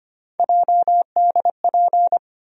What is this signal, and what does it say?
Morse code, heard as a keyed tone, about 25 wpm, 700 Hz. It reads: JDP
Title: USA Call Sign Postfixes - 3 Letters 25wpm